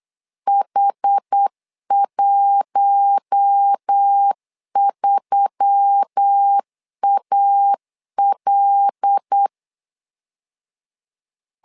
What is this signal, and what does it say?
Morse code, heard as a keyed tone, about 8 wpm, 800 Hz. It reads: H13AL
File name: LEZIONI / 5 LBVC / 5 LB 8wpm 2816